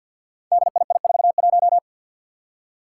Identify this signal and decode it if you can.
Morse code, heard as a keyed tone, about 52 wpm, 700 Hz. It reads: DII41
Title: Five Challenging Letter-Number Combinations 50wpm